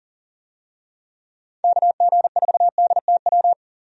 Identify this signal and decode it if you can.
Morse code, heard as a keyed tone, about 40 wpm, 700 Hz. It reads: KG4DTW
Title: USA Call Signs - 6 Characters - 40wpm